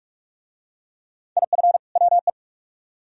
Morse code, heard as a keyed tone, about 44 wpm, 700 Hz. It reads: IF WE